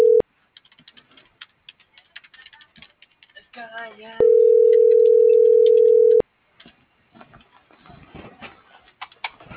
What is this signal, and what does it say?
Ringback tone (North America), 440 + 480 Hz.
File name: CENTRAL MABE / out maria